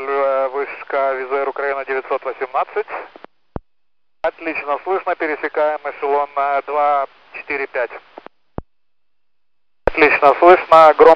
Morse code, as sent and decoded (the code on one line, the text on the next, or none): none